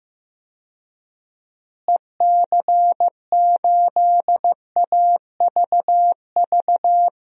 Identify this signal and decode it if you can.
Morse code, heard as a keyed tone, about 16 wpm, 700 Hz. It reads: EC8AVV
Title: DXCC-100 Call Signs - 6 Characters - Repeated 3x - 15wpm